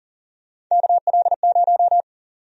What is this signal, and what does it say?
Morse code, heard as a keyed tone, about 40 wpm, 700 Hz. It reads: KL0
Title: USA Call Sign Prefixes - 40wpm